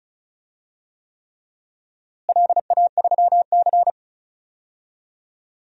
Morse code, heard as a keyed tone, about 36 wpm, 700 Hz. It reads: LA3C
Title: DXCC-100 Call Signs - 4 Characters - Repeated 3x - 35wpm